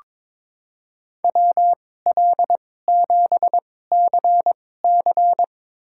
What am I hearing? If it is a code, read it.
Morse code, heard as a keyed tone, about 22 wpm, spaced as for 16 wpm, 700 Hz. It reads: WL7CC
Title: USA Call Signs - 5 Characters - 2x Character Spacing - 22wpm